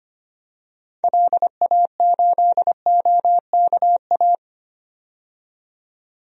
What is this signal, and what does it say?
Morse code, heard as a keyed tone, about 25 wpm, 700 Hz. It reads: LA8OKA
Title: DXCC-100 Call Signs - 6 Characters - 25wpm